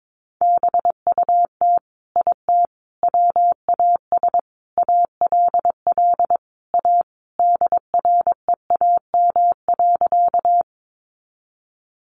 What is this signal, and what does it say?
Morse code, heard as a keyed tone, about 22 wpm, 700 Hz. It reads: BUT IT WAS ALL A DREAM.
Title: Sentences from Top 1500 Words 22wpm